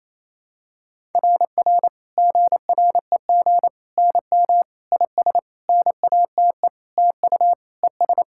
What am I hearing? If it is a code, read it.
Morse code, heard as a keyed tone, about 28 wpm, 700 Hz. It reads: RR GREG NM IS NATE TU ES
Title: SOTA QSO Practice 28wpm